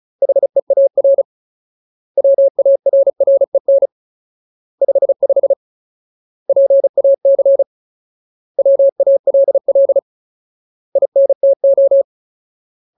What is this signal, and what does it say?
Morse code, heard as a keyed tone, about 34 wpm, 550 Hz. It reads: HEAR WARREN 55 PAC WALL INTO
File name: Ultimate Word List / 35WPM Ultimate